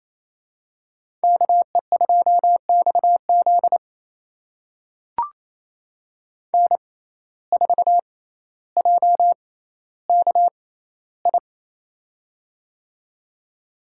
Morse code, heard as a keyed tone, about 28 wpm, spaced as for 14 wpm, 700 Hz. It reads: KE2XZ N4JKI
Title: USA Call Signs - 5 Characters - 6x Character Spacing - 28wpm